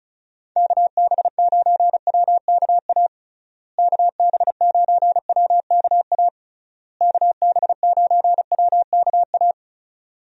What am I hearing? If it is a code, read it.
Morse code, heard as a keyed tone, about 36 wpm, 700 Hz. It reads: KB9WKA KB9WKA KB9WKA
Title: USA Call Signs - 6 Characters - Repeated 3x - 35wpm